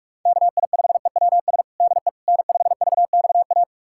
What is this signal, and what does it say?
Morse code, heard as a keyed tone, about 46 wpm, 700 Hz. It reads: KI5EWS DE N5VXA
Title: USA Callsign DE Callsign 45wpm